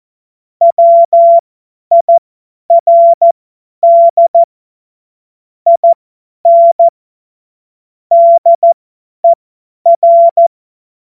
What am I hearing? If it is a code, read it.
Morse code, heard as a keyed tone, about 14 wpm, spaced as for 10 wpm, 680 Hz. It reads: WIRD IN DER
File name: Morsekurs / Witze / Witz 14. März 2023 50 F70